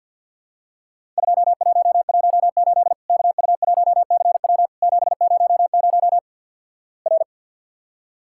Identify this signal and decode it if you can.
Morse code, heard as a keyed tone, about 50 wpm, 700 Hz. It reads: 2118 KU1CW 700 R
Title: Worked All Europe DX Contest - QTC Exchange with Repeats - 50wpm